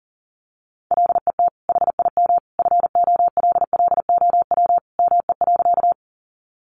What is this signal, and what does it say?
Morse code, heard as a keyed tone, about 40 wpm, 700 Hz. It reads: LET HIM FOLLOW ME.
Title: Sentences from Top 300 Words 40wpm